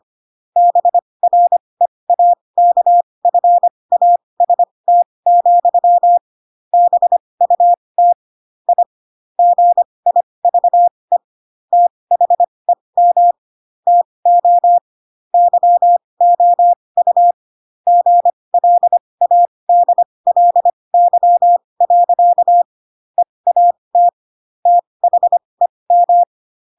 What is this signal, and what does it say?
Morse code, heard as a keyed tone, about 25 wpm, spaced as for 20 wpm, 700 Hz. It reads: BREAKFAST, BUT I GIVE THEM TO YOU GLADLY. EAT THEM
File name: Pinocchio / Pinocchio ch7 15 25 ik1hge